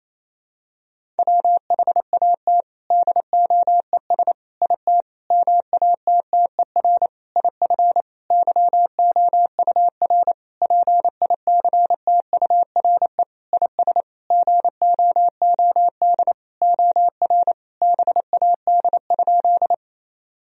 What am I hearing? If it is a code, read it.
Morse code, heard as a keyed tone, about 28 wpm, 700 Hz. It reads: WHAT DOES IT MATTER IF YOUR PICTURE IS GOOD OR BAD?
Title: Sentences from Top 1000 Words - Review Part 1 - 28wpm